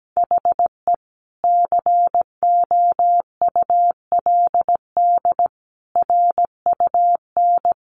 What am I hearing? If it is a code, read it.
Morse code, heard as a keyed tone, about 17 wpm, 700 Hz. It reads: HE COULD RUN
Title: Sentences from Top 300 Words 17wpm